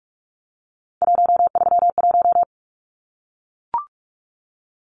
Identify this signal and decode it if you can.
Morse code, heard as a keyed tone, about 44 wpm, 700 Hz. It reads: 131
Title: Numbers - 3-Digit 45wpm